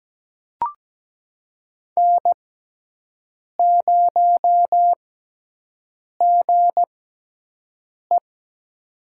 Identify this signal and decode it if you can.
Morse code, heard as a keyed tone, about 17 wpm, spaced as for 6 wpm, 700 Hz. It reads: N0GE
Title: USA Call Signs - 4 Characters - 6x Character Spacing - 17wpm